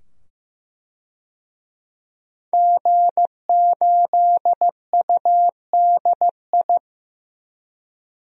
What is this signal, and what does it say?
Morse code, heard as a keyed tone, about 15 wpm, 700 Hz. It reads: G8UDI
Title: DXCC-100 Call Signs - 5 Characters - 15wpm